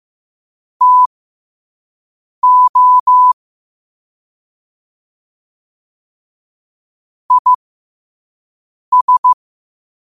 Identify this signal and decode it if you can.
Morse code, heard as a keyed tone, about 15 wpm, spaced as for 5 wpm, 1000 Hz. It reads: TO IS
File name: qotdfiles / 5wpm